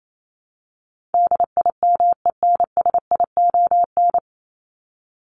Morse code, heard as a keyed tone, about 28 wpm, 700 Hz. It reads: DIMENSION
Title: Top 1501-1750 Words 28wpm